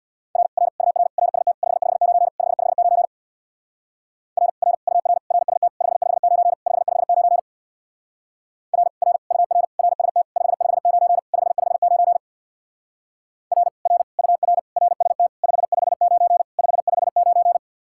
Morse code, heard as a keyed tone, about 61 wpm, 700 Hz. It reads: R R UR RST 559 559 R R UR RST 559 559 R R UR RST 559 559 R R UR RST 559 559
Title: Speed-Racing Common QSO Phrases - 50wpm